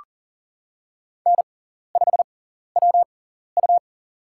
Morse code, heard as a keyed tone, about 39 wpm, spaced as for 14 wpm, 700 Hz. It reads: N5WU
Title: USA Call Signs - 6x Character Spacing - 40wpm